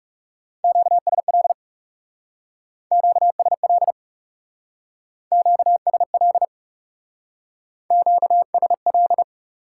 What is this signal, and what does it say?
Morse code, heard as a keyed tone, about 36 wpm, 700 Hz. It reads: QSL QSL QSL QSL
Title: Speed-Racing Top 90 QSO Elements 30wpm